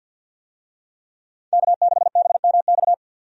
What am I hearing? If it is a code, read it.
Morse code, heard as a keyed tone, about 49 wpm, 700 Hz. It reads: K6BMX
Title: USA Call Signs - 5 Characters - 50wpm